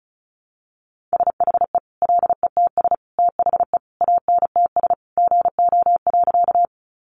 Morse code, heard as a keyed tone, about 34 wpm, 700 Hz. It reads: SHE LETS THE ANTS GO.
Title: Additional Easy Phrases 35wpm